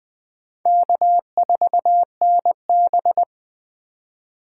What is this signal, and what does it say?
Morse code, heard as a keyed tone, about 24 wpm, 700 Hz. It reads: K4NB
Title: Speed Racing - USA 1x2 Call Signs 20wpm